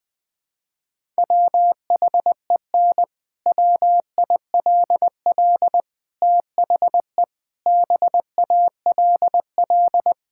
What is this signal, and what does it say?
Morse code, heard as a keyed tone, about 20 wpm, 700 Hz. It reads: WHEN WILL THE BALL
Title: Sentences from Top 1000 Words - Review Part 6 - 20wpm